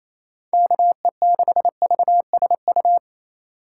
Morse code, heard as a keyed tone, about 28 wpm, 700 Hz. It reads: KE6VSU
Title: USA Call Signs - 6 Characters - 3x Character Spacing - 28wpm